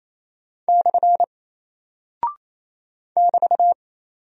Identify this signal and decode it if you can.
Morse code, heard as a keyed tone, about 29 wpm, 700 Hz. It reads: / <BT>
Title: Single Character (Letter:Number:Punctuation:Prosign) 28wpm